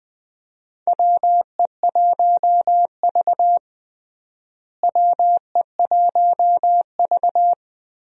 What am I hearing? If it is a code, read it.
Morse code, heard as a keyed tone, about 21 wpm, 700 Hz. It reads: WE1V WE1V
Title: USA Call Signs - 4 Characters - Repeated 3x - 20wpm